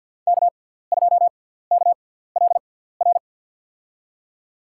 Morse code, heard as a keyed tone, about 52 wpm, spaced as for 17 wpm, 690 Hz. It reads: K2KLR
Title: USA Call Signs - 5 Characters - 6x Character Spacing - 50wpm